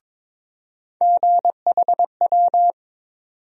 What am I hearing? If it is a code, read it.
Morse code, heard as a keyed tone, about 22 wpm, 700 Hz. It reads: GHW
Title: USA Call Sign Postfixes - 3 Letters 22wpm